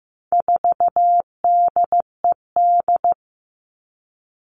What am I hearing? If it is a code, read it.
Morse code, heard as a keyed tone, about 15 wpm, 700 Hz. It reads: 4DED
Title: All Cut Numbers - 4-Digit 15wpm